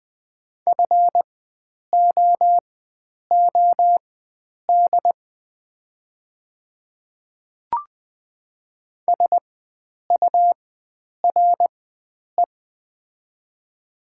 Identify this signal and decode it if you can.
Morse code, heard as a keyed tone, about 20 wpm, spaced as for 9 wpm, 700 Hz. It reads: FOOD SURE
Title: 4x Character Spacing - Top 300 Words - 20wpm